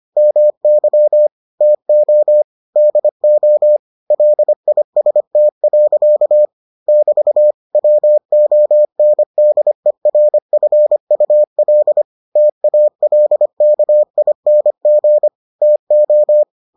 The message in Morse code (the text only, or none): MY TO DO LIST. <BT> WONDERFUL TALKING TO